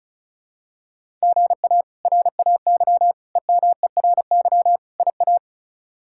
Morse code, heard as a keyed tone, about 34 wpm, 700 Hz. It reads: GA RAY EMERY IA
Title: K1USN SST QSO Practice - 35wpm